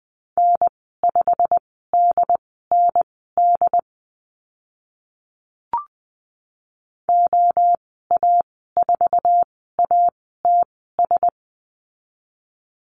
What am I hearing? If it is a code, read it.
Morse code, heard as a keyed tone, about 20 wpm, spaced as for 14 wpm, 700 Hz. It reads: N5DND OA4ATS
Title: 088 Lesson 15g - Number 5 - Call Signs 20wpm